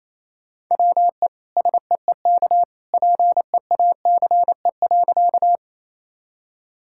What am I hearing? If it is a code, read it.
Morse code, heard as a keyed tone, about 28 wpm, 700 Hz. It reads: WE SEEK PEACE.
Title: Sentences from Top 1600 Words 28wpm